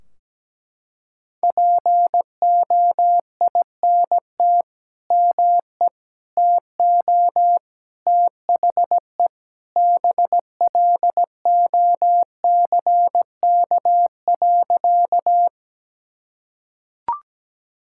Morse code, heard as a keyed tone, about 17 wpm, 700 Hz. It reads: POINT ME TO THE BLOCK.
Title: Sentences from Top 1600 Words 17wpm